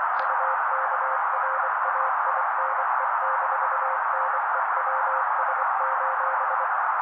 Morse code, heard as a keyed tone, about 24 wpm, 530 Hz. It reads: UKRAINE-NEWS:
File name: daily / 23wpm noise